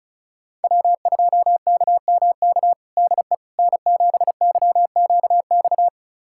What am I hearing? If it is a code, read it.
Morse code, heard as a keyed tone, about 36 wpm, 700 Hz. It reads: W2KMK DE N7YQX
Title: USA Callsign DE Callsign - 5 Characters - 35wpm